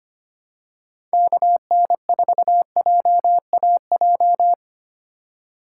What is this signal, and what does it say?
Morse code, heard as a keyed tone, about 25 wpm, 700 Hz. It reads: KN4JAJ
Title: USA Call Signs - 6 Characters - 2x Character Spacing - 25wpm